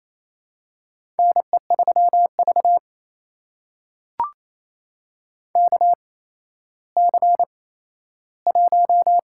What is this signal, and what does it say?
Morse code, heard as a keyed tone, about 28 wpm, spaced as for 14 wpm, 700 Hz. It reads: NE3V KC1
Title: USA Call Signs - 4 Characters - 8x Character Spacing - 28wpm